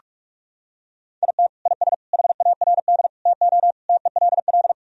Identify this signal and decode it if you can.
Morse code, heard as a keyed tone, about 42 wpm, 700 Hz. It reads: IT IS HARD TO TELL
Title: Sentences from Top 1000 Words - Review Part 3 - 45wpm